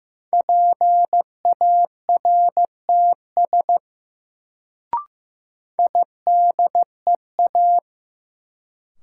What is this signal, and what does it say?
Morse code, heard as a keyed tone, about 15 wpm, 700 Hz. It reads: PARTS IDEA
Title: Sets of 3 Words - New Word Drill - 15wpm